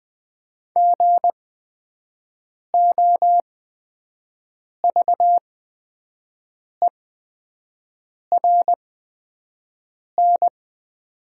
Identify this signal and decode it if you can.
Morse code, heard as a keyed tone, about 20 wpm, spaced as for 5 wpm, 700 Hz. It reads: GOVERN
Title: 8x Character Spacing - Top 500 Words - 20wpm